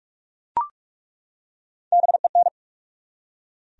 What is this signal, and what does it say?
Morse code, heard as a keyed tone, about 47 wpm, 700 Hz. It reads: BEN
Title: USA Common Names - Max 3 Letters - 45wpm